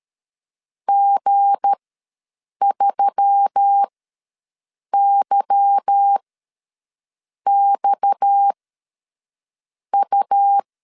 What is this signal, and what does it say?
Morse code, heard as a keyed tone, about 13 wpm, spaced as for 6 wpm, 800 Hz. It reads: G3YXU
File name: LEZIONI / 8 YJ45 / 8 Y ws 12wpm 3409